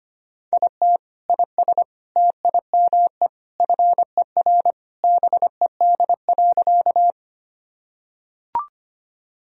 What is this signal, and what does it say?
Morse code, heard as a keyed tone, about 25 wpm, 700 Hz. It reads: IT IS TIME FER BED.
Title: Sentences from Top 500 Words and Abbreviations 25wpm